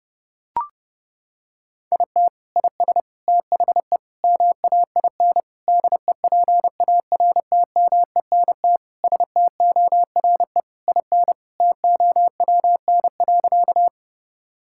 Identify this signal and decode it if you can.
Morse code, heard as a keyed tone, about 30 wpm, 700 Hz. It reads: IT IS THE MAIN DEPARTMENT STORE IN TOWN.
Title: Sentences from Top 1000 Words - Review Part 5 - 30wpm